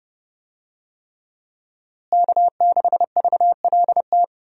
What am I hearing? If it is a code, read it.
Morse code, heard as a keyed tone, about 32 wpm, 700 Hz. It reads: K6VLT
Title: USA Call Signs - 5 Characters - Repeated 3x - 30wpm